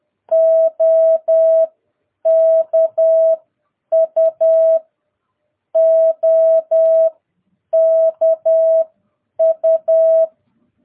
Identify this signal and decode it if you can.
Morse code, heard as a keyed tone, about 10 wpm, spaced as for 8 wpm, 660 Hz. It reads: OKU OKU
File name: jh8cbh / 2022cw